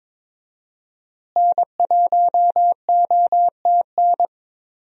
Morse code, heard as a keyed tone, about 22 wpm, 700 Hz. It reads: N1OTN
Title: USA Call Signs - 5 Characters - Repeated 2x - 22wpm